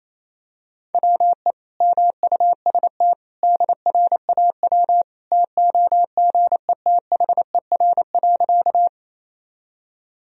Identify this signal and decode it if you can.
Morse code, heard as a keyed tone, about 28 wpm, 700 Hz. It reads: WE MUST DRAW TOGETHER.
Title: Sentences from Top 1000 Words - Review Part 1 - 28wpm